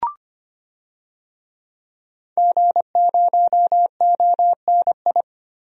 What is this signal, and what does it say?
Morse code, heard as a keyed tone, about 25 wpm, 700 Hz. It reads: G0ONI
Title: DXCC-100 Call Signs - 5 Characters - 25wpm